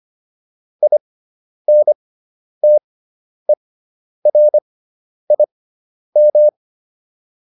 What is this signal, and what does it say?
Morse code, heard as a keyed tone, about 25 wpm, spaced as for 10 wpm, 600 Hz. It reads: INTERIM